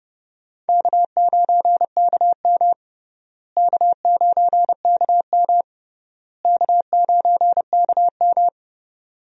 Morse code, heard as a keyed tone, about 29 wpm, 700 Hz. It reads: K9KM K9KM K9KM
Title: USA Call Signs - 4 Characters - Repeated 3x - 30wpm